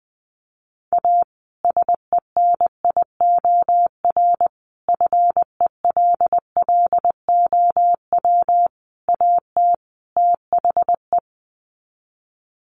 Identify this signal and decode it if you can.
Morse code, heard as a keyed tone, about 20 wpm, 700 Hz. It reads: A SENIOR FELLOW AT THE
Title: Sets of 5 Words Encore 20wpm